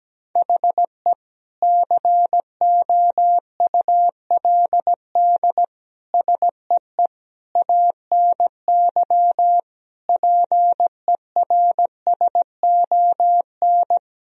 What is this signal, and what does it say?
Morse code, heard as a keyed tone, about 17 wpm, 700 Hz. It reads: HE COULD SEE ANY PERSON
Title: Sentences from Top 1000 Words - Review Part 3 - 17wpm